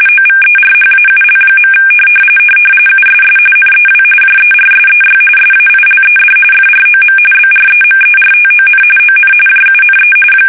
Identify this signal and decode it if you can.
RTTY, about 45.45 baud. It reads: OVERHAUL. USS NEW JERSEY SAILED FROM BREMERTON ON JUNE
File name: fsk850 / battleshipnj